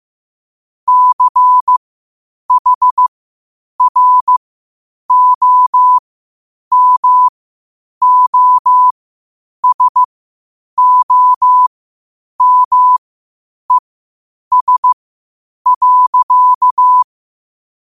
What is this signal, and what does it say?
Morse code, heard as a keyed tone, about 15 wpm, spaced as for 8 wpm, 1000 Hz. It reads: CHROMOSOMES.
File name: qotdfiles / 8wpm